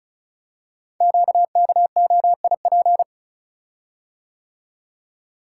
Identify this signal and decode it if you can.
Morse code, heard as a keyed tone, about 33 wpm, 690 Hz. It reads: QKOIP
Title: Code Groups - 5 Letters - 35wpm